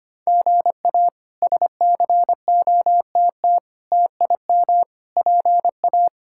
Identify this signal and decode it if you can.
Morse code, heard as a keyed tone, about 25 wpm, 700 Hz. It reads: GA SCOTT TIM PA
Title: K1USN SST QSO Practice - 25wpm